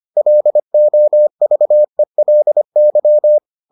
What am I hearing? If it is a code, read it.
Morse code, heard as a keyed tone, about 25 wpm, 600 Hz. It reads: LOVELY